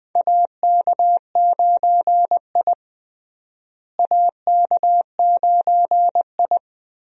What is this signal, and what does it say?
Morse code, heard as a keyed tone, about 21 wpm, 700 Hz. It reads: AK9I AK9I
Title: USA Call Signs - 4 Characters - Repeated 3x - 20wpm